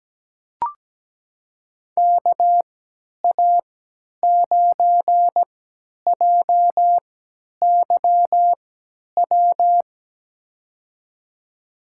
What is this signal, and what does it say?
Morse code, heard as a keyed tone, about 17 wpm, spaced as for 10 wpm, 700 Hz. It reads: KA9JYW
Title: USA Call Signs - 3x Character Spacing - 17wpm